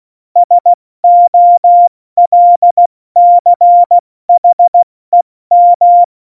Morse code, heard as a keyed tone, about 16 wpm, 700 Hz. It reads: SOLCHEM
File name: Morsekurs / Witze / Witz 27 Juni 17, Tempo 70 F80